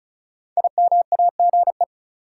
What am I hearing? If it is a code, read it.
Morse code, heard as a keyed tone, about 36 wpm, 700 Hz. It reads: IMAGE
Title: Sentences from Top 700 Words - New Word Drill - 35wpm